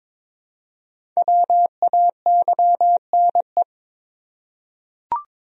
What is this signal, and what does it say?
Morse code, heard as a keyed tone, about 22 wpm, 700 Hz. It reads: WAYNE
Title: USA Common Names - Max 5 Letters - 22wpm